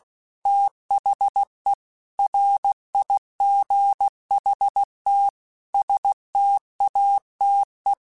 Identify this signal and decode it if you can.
Morse code, heard as a keyed tone, about 16 wpm, 800 Hz. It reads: THE RIGHT STATE
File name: Technical Subjects / Technical Subjects-mp3s / OR Gate @15